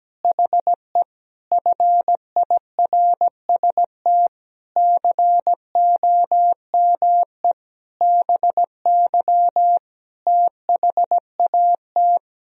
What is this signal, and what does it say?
Morse code, heard as a keyed tone, about 17 wpm, 700 Hz. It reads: HE FIRST COME BY THAT
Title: Sentences from Top 1000 Words - Review Part 2 - 17wpm